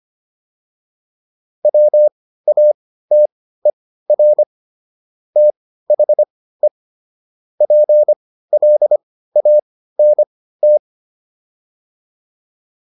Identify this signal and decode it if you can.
Morse code, heard as a keyed tone, about 25 wpm, spaced as for 15 wpm, 600 Hz. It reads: WATER THE PLANT